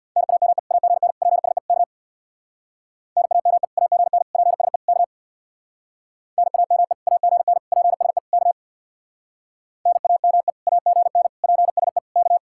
Morse code, heard as a keyed tone, about 57 wpm, 700 Hz. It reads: NAME AGN PSE K NAME AGN PSE K NAME AGN PSE K NAME AGN PSE K
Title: Speed-Racing Common QSO Phrases - 50wpm